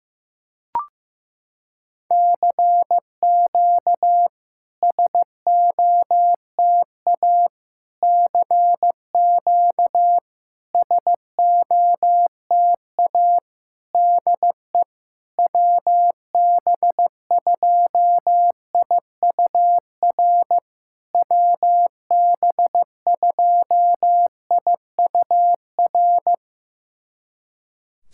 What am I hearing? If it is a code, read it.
Morse code, heard as a keyed tone, about 15 wpm, 700 Hz. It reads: CQ SOTA CQ SOTA DE WB2IUR WB2IUR
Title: SOTA QSO Practice 15wpm